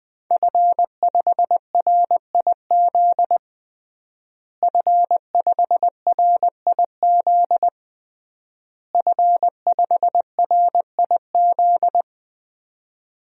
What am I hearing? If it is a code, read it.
Morse code, heard as a keyed tone, about 20 wpm, 700 Hz. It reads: F5RIZ F5RIZ F5RIZ
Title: DXCC-100 Call Signs - 5 Characters - Repeated 3x - 20wpm